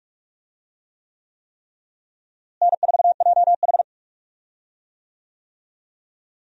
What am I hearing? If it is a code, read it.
Morse code, heard as a keyed tone, about 46 wpm, 700 Hz. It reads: N4JH
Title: USA Call Signs - 4 Characters - Repeated 3x - 45wpm